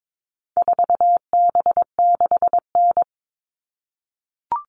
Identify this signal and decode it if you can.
Morse code, heard as a keyed tone, about 22 wpm, 700 Hz. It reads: 4B6N
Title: All Cut Numbers - 4-Digit 22wpm